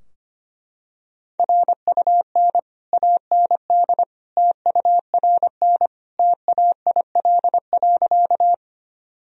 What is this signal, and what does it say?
Morse code, heard as a keyed tone, about 25 wpm, 700 Hz. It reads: RUN AND TURN TAIL.
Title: Sentences from Top 1000 Words - Review Part 3 - 25wpm